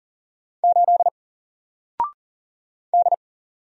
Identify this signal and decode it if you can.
Morse code, heard as a keyed tone, about 40 wpm, 700 Hz. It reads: 8 D
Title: Single Letter-Number 40wpm